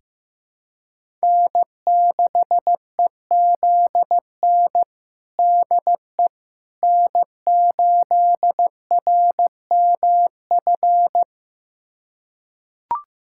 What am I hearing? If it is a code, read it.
Morse code, heard as a keyed tone, about 15 wpm, 700 Hz. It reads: N6EZN DE N8RMF
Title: USA Callsign DE Callsign - 5 Characters - 15wpm